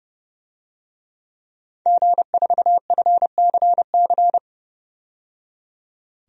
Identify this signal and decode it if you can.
Morse code, heard as a keyed tone, about 30 wpm, 700 Hz. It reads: G4FCC
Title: DXCC-100 Call Signs - 5 Characters - 30wpm